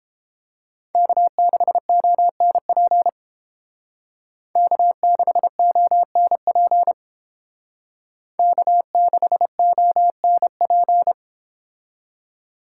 Morse code, heard as a keyed tone, about 30 wpm, 700 Hz. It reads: K6ONP K6ONP K6ONP